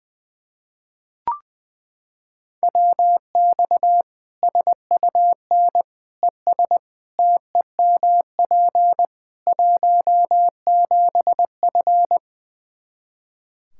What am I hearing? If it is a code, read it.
Morse code, heard as a keyed tone, about 20 wpm, 700 Hz. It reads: WX SUN ES TEMP 17F
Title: K1USN SST QSO Practice 20wpm